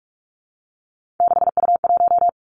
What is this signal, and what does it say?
Morse code, heard as a keyed tone, about 46 wpm, 700 Hz. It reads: 6U1
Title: Three Challenging Letter-Number Combinations 45wpm